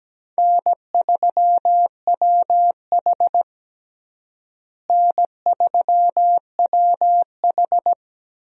Morse code, heard as a keyed tone, about 17 wpm, 700 Hz. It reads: N3WH N3WH
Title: USA Call Signs - 4 Characters - Repeated 3x - 17wpm